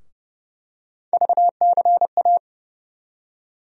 Morse code, heard as a keyed tone, about 30 wpm, 700 Hz. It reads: VCA
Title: USA Call Sign Postfixes - 3 Letters 30wpm